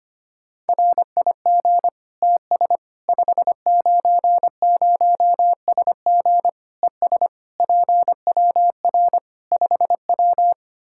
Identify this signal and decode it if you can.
Morse code, heard as a keyed tone, about 25 wpm, 700 Hz. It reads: RIG TS 590SG ES PWR 5W
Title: Practice QSO - Exchange 2 - Info - 25wpm